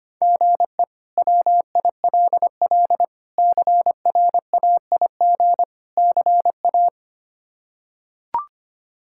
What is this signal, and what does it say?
Morse code, heard as a keyed tone, about 25 wpm, 700 Hz. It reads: GE WILL CRAIG CA
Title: K1USN SST QSO Practice - 25wpm